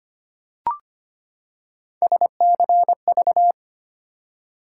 Morse code, heard as a keyed tone, about 25 wpm, 700 Hz. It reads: SCV
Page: morse-code-ninja-practice-25wpm